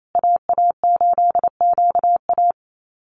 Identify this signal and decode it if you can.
Morse code, heard as a keyed tone, about 28 wpm, 700 Hz. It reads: AA8QA
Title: USA Call Signs Repeated 3x 28wpm